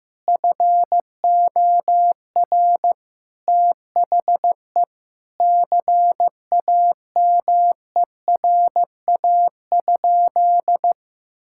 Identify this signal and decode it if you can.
Morse code, heard as a keyed tone, about 15 wpm, 700 Hz. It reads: FOR THE CAMERA?
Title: Sentences from Top 1000 Words - Review Part 5 - 15wpm